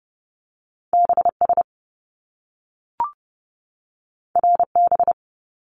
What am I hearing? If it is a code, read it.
Morse code, heard as a keyed tone, about 30 wpm, 700 Hz. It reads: BS RB
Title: Top Two Letter Combinations in Words 30wpm